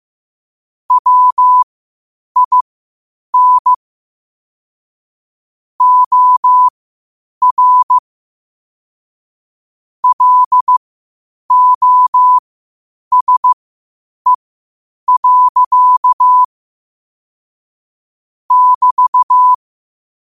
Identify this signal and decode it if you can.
Morse code, heard as a keyed tone, about 15 wpm, spaced as for 8 wpm, 1000 Hz. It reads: WIN OR LOSE. <BT>
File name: qotdfiles / 8wpm